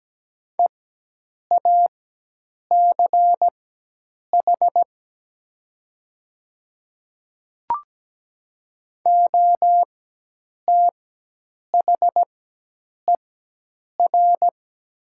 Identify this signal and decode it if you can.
Morse code, heard as a keyed tone, about 17 wpm, spaced as for 8 wpm, 700 Hz. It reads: EACH OTHER
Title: 4x Character Spacing - Top 100 Words - 17wpm